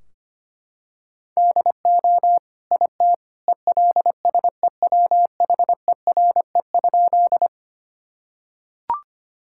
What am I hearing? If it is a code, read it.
Morse code, heard as a keyed tone, about 25 wpm, 700 Hz. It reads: DO IT ELSEWHERE?
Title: Sentences from Top 1600 Words 25wpm